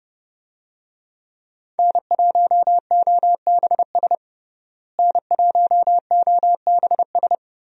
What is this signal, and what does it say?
Morse code, heard as a keyed tone, about 30 wpm, 700 Hz. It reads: N1OBS N1OBS
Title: USA Call Signs - 5 Characters - Repeated 3x - 30wpm